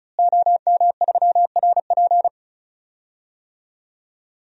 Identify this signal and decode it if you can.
Morse code, heard as a keyed tone, about 36 wpm, 700 Hz. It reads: OM3RP
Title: DXCC-100 Call Signs - 5 Characters - Repeated 3x - 35wpm